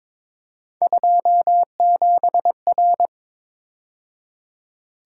Morse code, heard as a keyed tone, about 26 wpm, 700 Hz. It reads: 27R
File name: Speed-Racing - Three Letter-Number Combinations - 22wpm